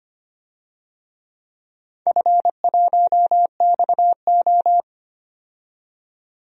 Morse code, heard as a keyed tone, about 25 wpm, 700 Hz. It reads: F1XO
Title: DXCC-100 Call Signs - 4 Characters - 25wpm